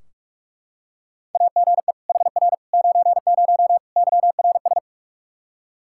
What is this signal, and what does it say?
Morse code, heard as a keyed tone, about 44 wpm, 700 Hz. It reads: AGE HR 90 YRS
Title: Practice QSO - Exchange 3 - More Info - 45wpm